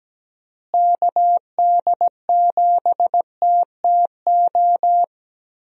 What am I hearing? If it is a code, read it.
Morse code, heard as a keyed tone, about 17 wpm, 700 Hz. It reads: KD7TTO
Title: USA Call Signs - 6 Characters - Repeated 3x - 17wpm